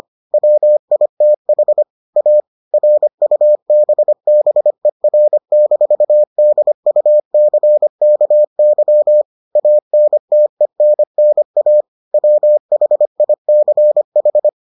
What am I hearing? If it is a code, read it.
Morse code, heard as a keyed tone, about 25 wpm, 600 Hz. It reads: WITH A RUBBER-DUCKY ANTENNA WHICH